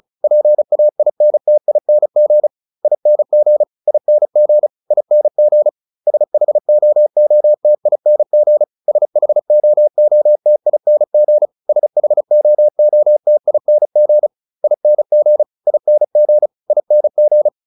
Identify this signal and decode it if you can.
Morse code, heard as a keyed tone, about 34 wpm, 600 Hz. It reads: PAINTING ING ING ING SHOOTING SHOOTING SHOOTING ING ING ING